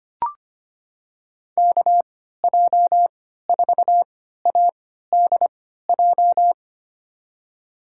Morse code, heard as a keyed tone, about 25 wpm, spaced as for 14 wpm, 700 Hz. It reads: KJ4ADJ
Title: USA Call Signs - 6 Characters - 3x Character Spacing - 25wpm